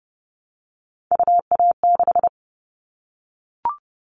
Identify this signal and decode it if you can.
Morse code, heard as a keyed tone, about 30 wpm, 700 Hz. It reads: UA6
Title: Callsign Prefixes for Top 25 DXCC - 30wpm